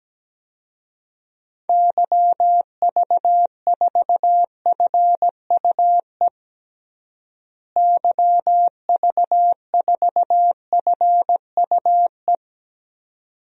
Morse code, heard as a keyed tone, about 17 wpm, 700 Hz. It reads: YV4FUE YV4FUE
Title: DXCC-100 Call Signs - 6 Characters - Repeated 2x - 17wpm